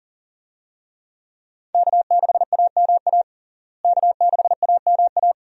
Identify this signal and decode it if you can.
Morse code, heard as a keyed tone, about 40 wpm, 700 Hz. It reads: K6AMA K6AMA
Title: USA Call Signs - 5 Characters - Repeated 2x - 40wpm